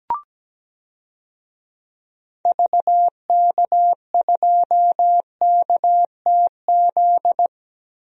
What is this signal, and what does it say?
Morse code, heard as a keyed tone, about 17 wpm, 700 Hz. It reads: VK2KTZ
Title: DXCC-100 Call Signs - 6 Characters - Repeated 2x - 17wpm